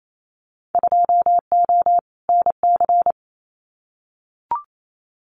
Morse code, heard as a keyed tone, about 28 wpm, 700 Hz. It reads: 2O NC
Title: Winter Field Day Exchanges 28wpm